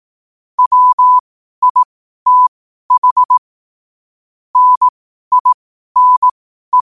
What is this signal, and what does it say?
Morse code, heard as a keyed tone, about 18 wpm, spaced as for 12 wpm, 1000 Hz. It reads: WITH NINE
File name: qotdfiles / 12wpm